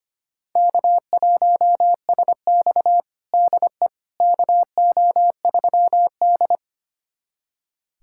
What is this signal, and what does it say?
Morse code, heard as a keyed tone, about 25 wpm, 700 Hz. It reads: K1SX DE KO3D
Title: USA Callsign DE Callsign - 4 Characters - 25wpm